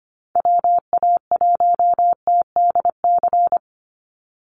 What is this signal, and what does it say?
Morse code, heard as a keyed tone, about 25 wpm, 700 Hz. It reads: WA1TDC
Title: USA Call Signs 25wpm